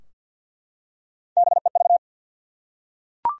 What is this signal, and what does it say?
Morse code, heard as a keyed tone, about 50 wpm, 700 Hz. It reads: BEV
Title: USA Common Names - Max 3 Letters - 50wpm